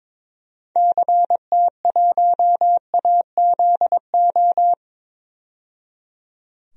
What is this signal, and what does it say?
Morse code, heard as a keyed tone, about 22 wpm, 700 Hz. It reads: CT1AZO
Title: DXCC-100 Call Signs - 6 Characters - 22wpm